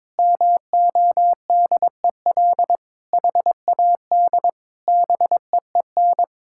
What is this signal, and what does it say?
Morse code, heard as a keyed tone, about 22 wpm, 700 Hz. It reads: MODEL HAD BEEN
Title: Sentences from Top 1000 Words - Review Part 5 - 22wpm